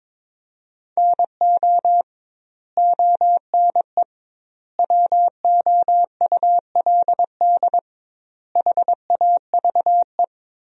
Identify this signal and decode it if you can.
Morse code, heard as a keyed tone, about 22 wpm, 700 Hz. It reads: NO ONE WOULD HAVE
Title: 2x Word-Spacing - Sentences from Top 300 Words - 22wpm